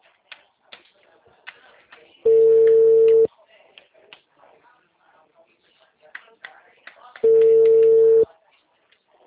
Ringback tone (China, 450 Hz).